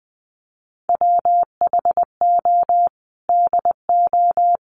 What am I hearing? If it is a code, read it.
Morse code, heard as a keyed tone, about 20 wpm, 700 Hz. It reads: WHO DO
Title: Sentences from Top 1500 Words 20wpm